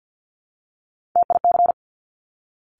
Morse code, heard as a keyed tone, about 50 wpm, 700 Hz. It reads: TIC
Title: Top Three Letter Combinations in Words 50wpm